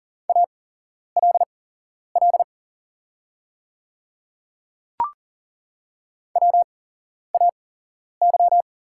Morse code, heard as a keyed tone, about 40 wpm, spaced as for 11 wpm, 690 Hz. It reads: ALL WAY
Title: 8x Character Spacing - Top 100 Words - 40wpm